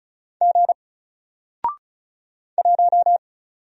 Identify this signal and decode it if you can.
Morse code, heard as a keyed tone, about 36 wpm, 700 Hz. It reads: G 1
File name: Single-Letter-Number-35wpm